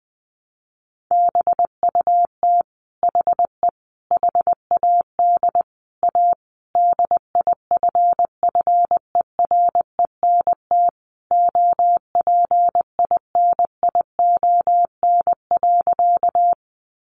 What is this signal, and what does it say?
Morse code, heard as a keyed tone, about 20 wpm, 700 Hz. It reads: BUT HE HAD A DIFFERENT OPINION.
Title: Sentences from Top 1200 Words 20wpm